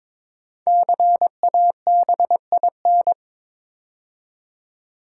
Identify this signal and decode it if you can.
Morse code, heard as a keyed tone, about 22 wpm, 700 Hz. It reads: CABIN
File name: 5 Letter Words - First 3 Letters Given - 22wpm